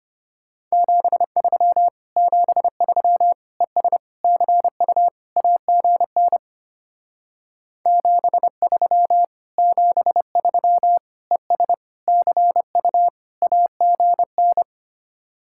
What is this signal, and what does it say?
Morse code, heard as a keyed tone, about 31 wpm, 700 Hz. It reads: 73 73 ES CU AGN 73 73 ES CU AGN
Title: Speed-Racing Common QSO Phrases - 25wpm